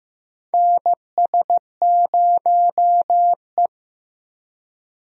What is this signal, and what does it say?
Morse code, heard as a keyed tone, about 15 wpm, 700 Hz. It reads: NS0E
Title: USA Call Signs - 4 Characters - Repeated 2x - 15wpm